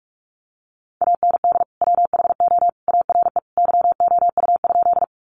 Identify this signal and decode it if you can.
Morse code, heard as a keyed tone, about 44 wpm, 700 Hz. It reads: AND WHO ARE YOU?
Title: Sentences from Top 100 Words 45wpm